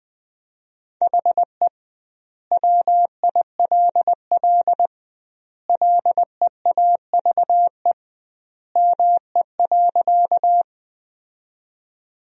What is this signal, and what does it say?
Morse code, heard as a keyed tone, about 21 wpm, 700 Hz. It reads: HE WILL LEAVE ME.
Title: 2x Word-Spacing - Sentences from Top 400 Words - 20wpm